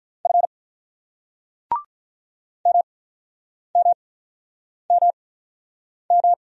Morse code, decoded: F MMMM

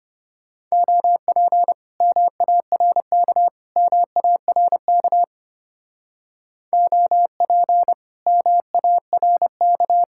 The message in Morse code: OP MARK MARK OP MARK

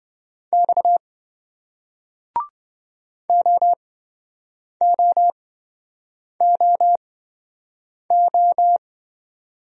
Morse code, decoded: X OOOO

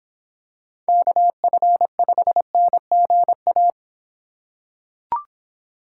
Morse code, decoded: KF5NGA

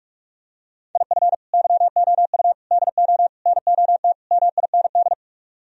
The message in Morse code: IF YOU DO NOT MIND

